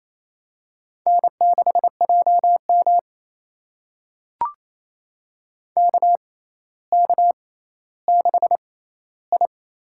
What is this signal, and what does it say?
Morse code, heard as a keyed tone, about 28 wpm, spaced as for 14 wpm, 700 Hz. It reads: N6JM KK6I
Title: USA Call Signs - 6x Character Spacing - 28wpm